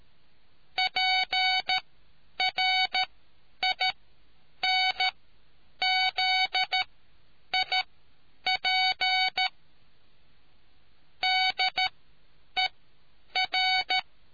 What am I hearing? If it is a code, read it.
Morse code, heard as a keyed tone, about 13 wpm, spaced as for 8 wpm, 750 Hz. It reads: PRINZIP DER